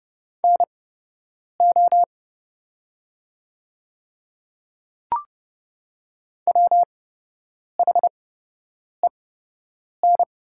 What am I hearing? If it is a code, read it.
Morse code, heard as a keyed tone, about 30 wpm, spaced as for 8 wpm, 700 Hz. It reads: NO WHEN
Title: 8x Character Spacing - Top 100 Words - 30wpm